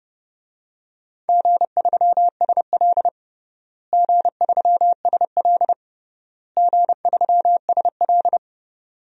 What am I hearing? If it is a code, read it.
Morse code, heard as a keyed tone, about 30 wpm, 700 Hz. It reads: G3SL G3SL G3SL
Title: DXCC-100 Call Signs - 4 Characters - Repeated 3x - 30wpm